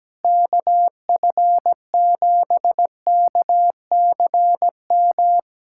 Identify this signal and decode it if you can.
Morse code, heard as a keyed tone, about 17 wpm, 700 Hz. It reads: KF7KCM
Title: USA Callsign DE Callsign - 6 Characters - 17wpm